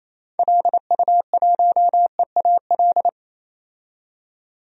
Morse code, heard as a keyed tone, about 28 wpm, 700 Hz. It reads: LU1EAL
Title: DXCC-100 Call Signs - 6 Characters - 28wpm